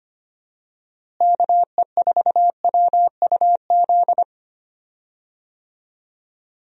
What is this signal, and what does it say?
Morse code, heard as a keyed tone, about 25 wpm, 700 Hz. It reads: KE4WUZ
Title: USA Call Signs - 6 Characters - Repeated 3x - 25wpm